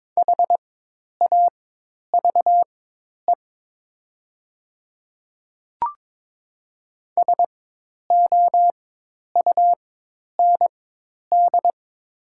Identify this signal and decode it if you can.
Morse code, heard as a keyed tone, about 22 wpm, spaced as for 10 wpm, 700 Hz. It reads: HAVE SOUND
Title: 4x Character Spacing - Top 200 Words - 22wpm